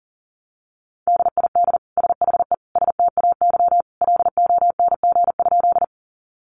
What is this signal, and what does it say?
Morse code, heard as a keyed tone, about 40 wpm, 700 Hz. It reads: DID SHE STAY LONG?
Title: Sentences from Top 500 Words 40wpm